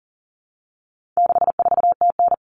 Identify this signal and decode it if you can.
Morse code, heard as a keyed tone, about 40 wpm, 700 Hz. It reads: 64TN
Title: All Cut Numbers - 4-Digit 40wpm